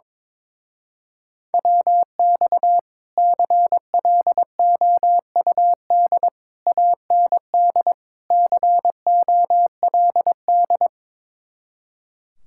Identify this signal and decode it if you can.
Morse code, heard as a keyed tone, about 22 wpm, 700 Hz. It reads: WX CLOUD AND COLD
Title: Speed-Racing Common QSO Phrases Encore - 22wpm